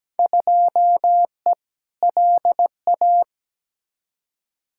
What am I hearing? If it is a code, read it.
Morse code, heard as a keyed tone, about 17 wpm, 700 Hz. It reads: 2E LA
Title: Field Day Exchanges Encore - 17wpm